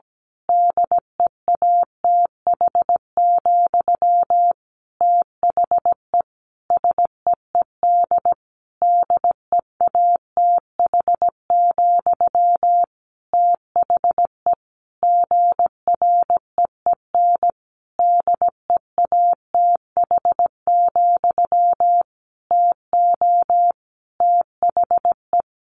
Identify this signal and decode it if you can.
Morse code, heard as a keyed tone, about 17 wpm, 700 Hz. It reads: DEATH, THE SEED DEATH, THE GREEN DEATH, TO THE
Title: A Sound Of Thunder 17wpm